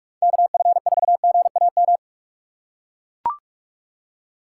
Morse code, heard as a keyed tone, about 41 wpm, 700 Hz. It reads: KF4GAM
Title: USA Call Signs - 4x Character Spacing - 45wpm